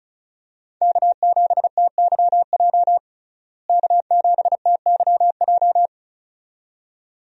Morse code, heard as a keyed tone, about 35 wpm, 700 Hz. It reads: K7TYJ K7TYJ
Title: USA Call Signs - 5 Characters - Repeated 2x - 35wpm